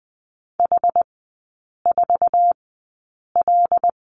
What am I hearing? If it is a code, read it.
Morse code, heard as a keyed tone, about 20 wpm, spaced as for 8 wpm, 700 Hz. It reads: H4L
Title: 152 Lesson 25e - Letter p - Three Characters 20wpm